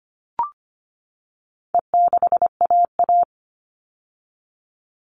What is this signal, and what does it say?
Morse code, heard as a keyed tone, about 25 wpm, 700 Hz. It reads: E6AA
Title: All Cut Numbers - 4-Digit 25wpm